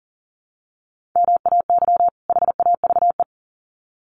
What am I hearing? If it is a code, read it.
Morse code, heard as a keyed tone, about 37 wpm, 700 Hz. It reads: MAY HAVE
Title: Sets of 2 Words 40wpm